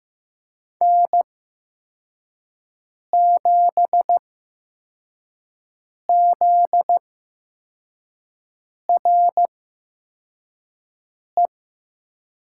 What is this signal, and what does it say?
Morse code, heard as a keyed tone, about 15 wpm, spaced as for 4 wpm, 700 Hz. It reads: N7ZRE
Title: USA Call Signs - 5 Characters - 8x Character Spacing - 15wpm